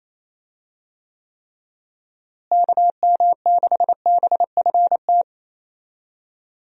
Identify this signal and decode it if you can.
Morse code, heard as a keyed tone, about 28 wpm, 700 Hz. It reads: KM6BFT
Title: USA Call Signs - 6 Characters - Repeated 3x - 28wpm